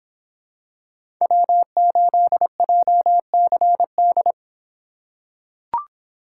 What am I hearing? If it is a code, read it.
Morse code, heard as a keyed tone, about 26 wpm, 700 Hz. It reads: W8JCD